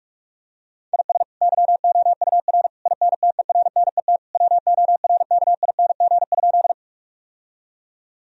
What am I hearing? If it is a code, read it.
Morse code, heard as a keyed tone, about 46 wpm, 700 Hz. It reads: IS YOUR INTERNET WORKING?
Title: Sentences from Top 1000 Words - Review Part 4 - 45wpm